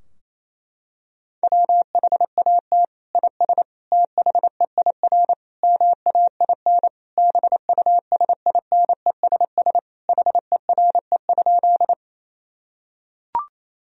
Morse code, heard as a keyed tone, about 27 wpm, 700 Hz. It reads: WHAT IS THEIR MAIN BUSINESS HERE?
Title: Sentences from Top 1000 Words - Review Part 2 - 28wpm